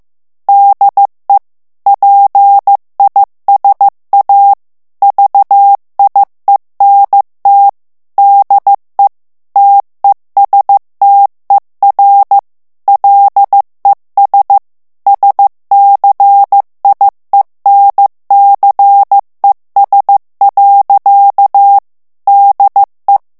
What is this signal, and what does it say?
Morse code, heard as a keyed tone, about 15 wpm, 800 Hz. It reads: DE PISA VIENT DE TESTER LES SCIENCES. DE
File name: morse mp3 / FRANCAIS TEXTE CLAIR 1 VITESSE 14 MOTS MINUTES